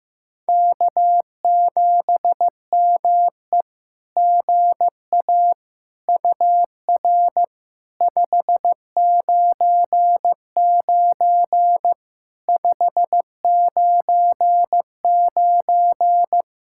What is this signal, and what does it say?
Morse code, heard as a keyed tone, about 16 wpm, 700 Hz. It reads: K7ME GA UR 599 599
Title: SOTA QSO Practice 15wpm